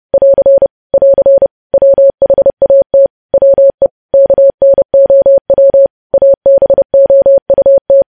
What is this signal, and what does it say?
Morse code, heard as a keyed tone, about 30 wpm, 550 Hz. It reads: <AR> <AR> WHAT WE KNOW ABOUT
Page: news headlines in morse code at 30 wpm